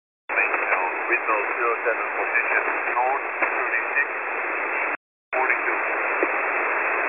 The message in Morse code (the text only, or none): none